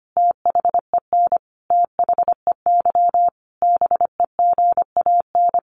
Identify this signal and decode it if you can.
Morse code, heard as a keyed tone, about 25 wpm, 700 Hz. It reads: THEN THEY BEGAN
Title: Sentences from Top 300 Words 25wpm